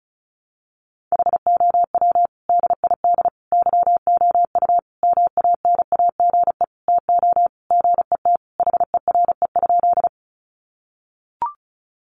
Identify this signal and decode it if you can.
Morse code, heard as a keyed tone, about 34 wpm, 700 Hz. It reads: HOW DID YOU MANAGE TO GET HERE?
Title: Sentences from Top 1400 Words 35wpm